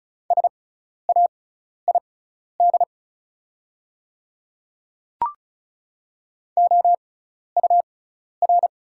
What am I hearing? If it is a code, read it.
Morse code, heard as a keyed tone, about 36 wpm, spaced as for 12 wpm, 690 Hz. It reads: SAID OUR
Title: 6x Character Spacing - Top 500 Words - 35wpm